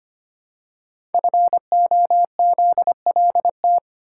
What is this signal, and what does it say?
Morse code, heard as a keyed tone, about 25 wpm, 700 Hz. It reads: FOZLT
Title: Code Groups - 5 Letters - 25wpm